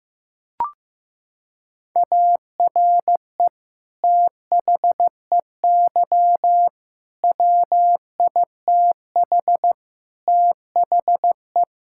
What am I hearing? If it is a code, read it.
Morse code, heard as a keyed tone, about 16 wpm, 700 Hz. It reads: ARE THEY WITH THE
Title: Sentences from Top 1000 Words - Review Part 5 - 15wpm